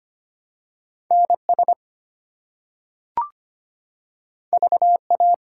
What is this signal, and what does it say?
Morse code, heard as a keyed tone, about 25 wpm, 700 Hz. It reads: NS VA
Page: morse-code-ninja-practice-25wpm